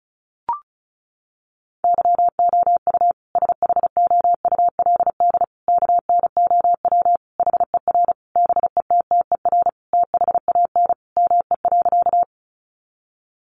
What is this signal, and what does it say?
Morse code, heard as a keyed tone, about 34 wpm, 700 Hz. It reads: YOU SHOULD KNOW HER BETTER THAN ME.
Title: Sentences from Top 400 Words 35wpm